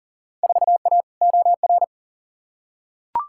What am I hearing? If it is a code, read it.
Morse code, heard as a keyed tone, about 38 wpm, 700 Hz. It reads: 4A OR
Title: Field Day Exchanges Encore - 40wpm